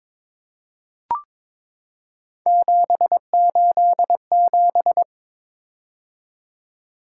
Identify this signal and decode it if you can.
Morse code, heard as a keyed tone, about 22 wpm, 700 Hz. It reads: 787
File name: practice / 22wpm / Three Letter-Number Combinations - 22wpm